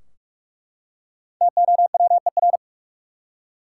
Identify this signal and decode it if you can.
Morse code, heard as a keyed tone, about 45 wpm, 700 Hz. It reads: TOWER
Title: Five Letter Words - 45wpm